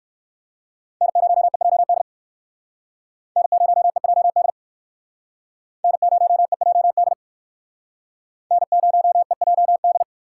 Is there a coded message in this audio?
Yes: Morse code from 1.0 to 10.0 s, heard as a keyed tone, about 59 wpm, 700 Hz. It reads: N0EJD N0EJD N0EJD N0EJD